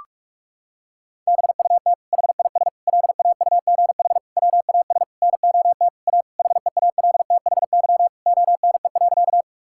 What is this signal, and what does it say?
Morse code, heard as a keyed tone, about 41 wpm, 700 Hz. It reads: BUT HIS LAUGH WAS NOT A HEALTHY ONE.
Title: Sentences from Top 1000 Words - Review Part 5 - 45wpm